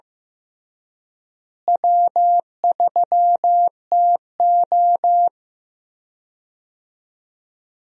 Morse code, heard as a keyed tone, about 15 wpm, 700 Hz. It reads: W3TO
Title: USA Call Signs - 4 Characters - Repeated 3x - 15wpm